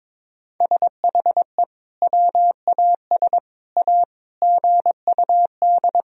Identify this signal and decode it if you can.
Morse code, heard as a keyed tone, about 22 wpm, 700 Hz. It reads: SHE WAS A GUD
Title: Sentences from Top 400 Words and Abbreviations 22wpm